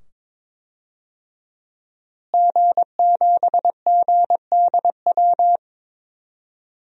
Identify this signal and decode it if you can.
Morse code, heard as a keyed tone, about 22 wpm, 700 Hz. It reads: G7GDW
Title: DXCC-100 Call Signs - 5 Characters - Repeated 2x - 22wpm